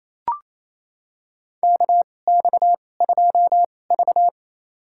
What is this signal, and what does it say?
Morse code, heard as a keyed tone, about 28 wpm, spaced as for 20 wpm, 700 Hz. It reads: KX2V
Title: USA Call Signs - 4 Characters - 2x Character Spacing - 28wpm